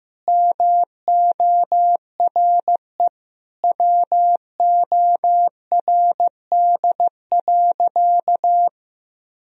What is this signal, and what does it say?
Morse code, heard as a keyed tone, about 15 wpm, 700 Hz. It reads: MORE WORD.
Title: Sentences from Top 1000 Words - Review Part 1 - 15wpm